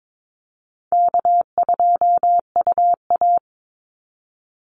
Morse code, heard as a keyed tone, about 22 wpm, 700 Hz. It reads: K2UA
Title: Easy Call Signs - 22wpm